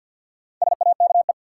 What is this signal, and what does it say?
Morse code, heard as a keyed tone, about 52 wpm, 700 Hz. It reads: SAKE
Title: Sets of 4 Words - New Word Drill - 50wpm